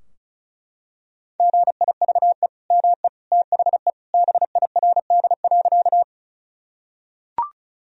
Morse code, heard as a keyed tone, about 36 wpm, 700 Hz. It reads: GIVE ME THE BIRD.
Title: Sentences from Top 1000 Words - Review Part 2 - 35wpm